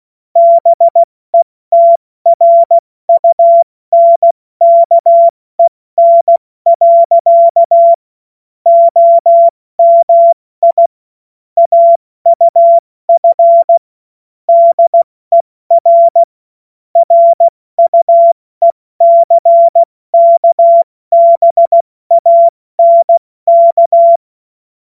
Morse code, heard as a keyed tone, about 16 wpm, 680 Hz. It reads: BETRUNKEN. OMI AUF DER RUECKBANK
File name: Morsekurs / Witze / Witz 26. Oktober 2021, 70 F80